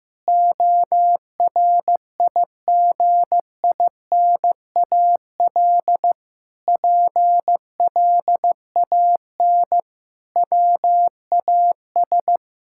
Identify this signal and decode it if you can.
Morse code, heard as a keyed tone, about 16 wpm, 700 Hz. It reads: ORIGINAL PLAN WAS
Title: Sentences from Top 1000 Words - Review Part 6 - 15wpm